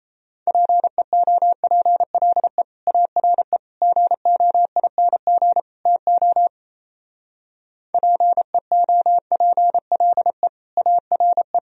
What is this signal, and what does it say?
Morse code, heard as a keyed tone, about 35 wpm, 700 Hz. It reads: PEOPLE ARE GOING TO PEOPLE ARE
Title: Speed-Racing - Sets of 4 Words 28wpm